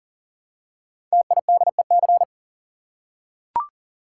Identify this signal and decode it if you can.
Morse code, heard as a keyed tone, about 40 wpm, 700 Hz. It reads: TIDEC
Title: Code Groups - 5 Letters - 40wpm